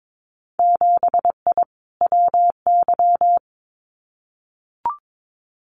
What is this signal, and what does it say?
Morse code, heard as a keyed tone, about 22 wpm, 700 Hz. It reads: 7I WY
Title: Winter Field Day Exchanges Encore 22wpm